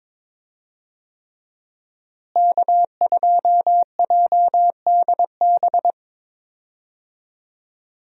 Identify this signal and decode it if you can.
Morse code, heard as a keyed tone, about 21 wpm, 700 Hz. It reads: K2JDB
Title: USA Call Signs - 5 Characters - Repeated 3x - 22wpm